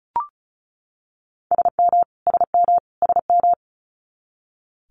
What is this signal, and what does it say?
Morse code, heard as a keyed tone, about 36 wpm, 700 Hz. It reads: SM SM SM
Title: Multi-letter Phonemes Repeated 3x 35wpm